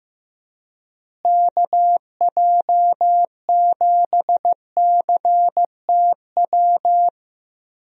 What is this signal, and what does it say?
Morse code, heard as a keyed tone, about 15 wpm, 700 Hz. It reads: KJ7CTW
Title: USA Call Signs - 6 Characters - Repeated 3x - 15wpm